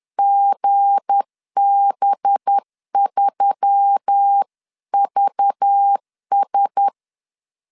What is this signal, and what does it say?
Morse code, heard as a keyed tone, about 10 wpm, 800 Hz. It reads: GB3VS
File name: LEZIONI / 7 GF9Z / 7 GF9 10wpm 2747